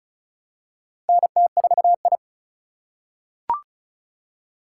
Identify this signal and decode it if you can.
Morse code, heard as a keyed tone, about 36 wpm, 700 Hz. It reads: NT4I
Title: USA Call Signs - 4 Characters - 3x Character Spacing - 35wpm